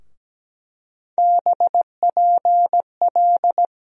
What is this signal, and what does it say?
Morse code, heard as a keyed tone, about 17 wpm, 700 Hz. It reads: BPL
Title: USA Call Sign Postfixes - 3 Letters 17wpm